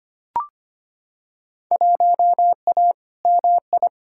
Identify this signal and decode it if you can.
Morse code, heard as a keyed tone, about 25 wpm, 700 Hz. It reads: 1A MI
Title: Field Day Exchanges - 25wpm